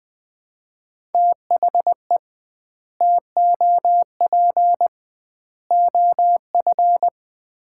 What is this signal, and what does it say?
Morse code, heard as a keyed tone, about 20 wpm, 700 Hz. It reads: THE TOP OF
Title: 2x Word-Spacing - Sets of 3 Words - 20wpm